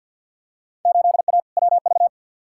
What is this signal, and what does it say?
Morse code, heard as a keyed tone, about 48 wpm, 700 Hz. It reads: 8A WV
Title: Field Day Exchanges Encore - 50wpm